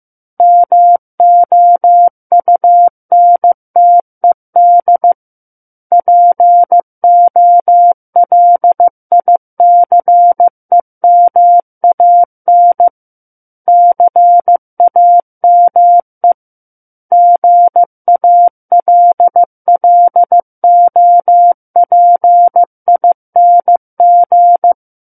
Morse code, heard as a keyed tone, about 15 wpm, 700 Hz. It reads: MOUNTED POLICEMAN CAME GALLOPING
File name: Chapter 6 - 15 WPM